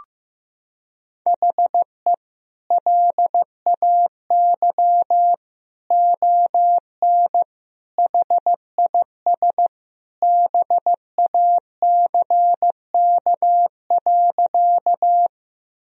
Morse code, heard as a keyed tone, about 15 wpm, 700 Hz. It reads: HE LAY ON HIS BACK.
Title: Sentences from Top 1000 Words - Review Part 2 - 15wpm